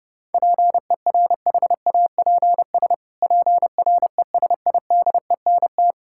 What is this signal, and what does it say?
Morse code, heard as a keyed tone, about 30 wpm, 700 Hz. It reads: PERHAPS PRESIDENT
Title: Pangram Challenge v2 - 30wpm